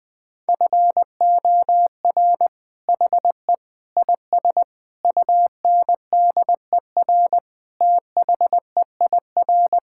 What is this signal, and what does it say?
Morse code, heard as a keyed tone, about 21 wpm, 700 Hz. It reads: FOR HE IS UNDER THEIR
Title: Sentences from Top 1600 Words 20wpm